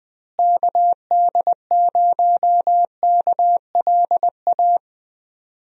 Morse code, heard as a keyed tone, about 20 wpm, 700 Hz. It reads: KD0KLA
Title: USA Call Signs - 6 Characters - Repeated 3x - 20wpm